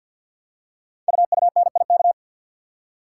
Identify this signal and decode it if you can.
Morse code, heard as a keyed tone, about 52 wpm, 700 Hz. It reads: UUNIK
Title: Code Groups - 5 Letters - 50wpm